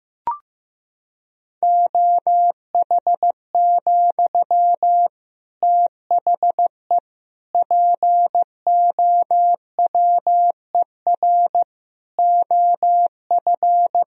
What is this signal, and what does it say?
Morse code, heard as a keyed tone, about 15 wpm, 700 Hz. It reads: OH, THE POWER OF
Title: Sentences from Top 1000 Words - Review Part 3 - 15wpm